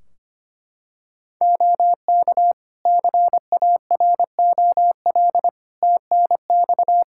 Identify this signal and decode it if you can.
Morse code, heard as a keyed tone, about 25 wpm, 700 Hz. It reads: OK CAROL TNX
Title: K1USN SST QSO Practice 25wpm